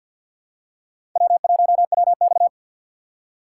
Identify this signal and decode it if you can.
Morse code, heard as a keyed tone, about 53 wpm, 700 Hz. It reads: W1WX
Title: USA Call Signs - 6x Character Spacing - 50wpm